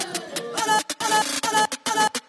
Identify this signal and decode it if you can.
There is no code.